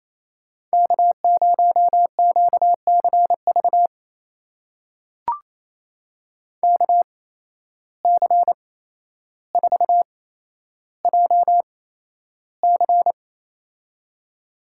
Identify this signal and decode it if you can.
Morse code, heard as a keyed tone, about 27 wpm, spaced as for 12 wpm, 700 Hz. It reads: K0QCV KC4JC
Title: USA Call Signs - 5 Characters - 8x Character Spacing - 28wpm